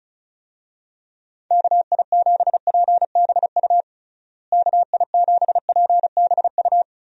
Morse code, heard as a keyed tone, about 36 wpm, 700 Hz. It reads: KI7PBU KI7PBU
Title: USA Call Signs - 6 Characters - Repeated 3x - 35wpm